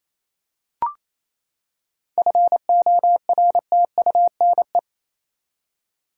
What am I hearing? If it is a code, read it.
Morse code, heard as a keyed tone, about 28 wpm, 700 Hz. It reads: FORTUNE
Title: Seven Letter Words - 28wpm